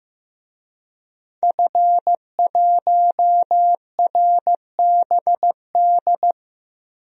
Morse code, heard as a keyed tone, about 15 wpm, 700 Hz. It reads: F1RBD
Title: DXCC-100 Call Signs - 5 Characters - Repeated 2x - 15wpm